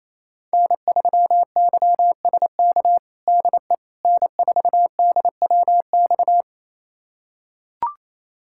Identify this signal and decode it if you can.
Morse code, heard as a keyed tone, about 28 wpm, 700 Hz. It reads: N3YSK DE N4DWX
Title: USA Callsign DE Callsign - 5 Characters - 28wpm